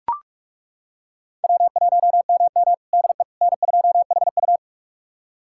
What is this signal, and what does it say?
Morse code, heard as a keyed tone, about 41 wpm, 700 Hz. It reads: W1MM DE N2HU
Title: USA Callsign DE Callsign - 4 Characters - 45wpm